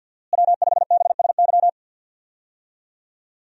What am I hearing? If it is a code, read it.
Morse code, heard as a keyed tone, about 48 wpm, 700 Hz. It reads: W5BSY
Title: USA Call Signs - 5 Characters - Repeated 2x - 50wpm